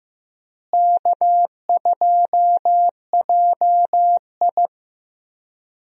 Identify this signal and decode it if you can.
Morse code, heard as a keyed tone, about 15 wpm, 700 Hz. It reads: K2JI
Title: USA Call Signs - 4 Characters - Repeated 3x - 15wpm